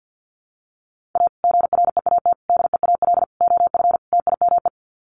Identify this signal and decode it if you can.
Morse code, heard as a keyed tone, about 50 wpm, 700 Hz. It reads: A GREAT DEAL OF TIME
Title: Sets of 5 Words Encore2 50wpm